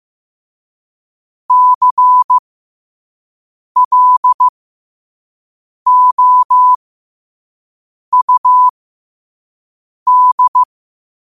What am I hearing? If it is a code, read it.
Morse code, heard as a keyed tone, about 15 wpm, spaced as for 5 wpm, 1000 Hz. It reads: CLOUD
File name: qotdfiles / 5wpm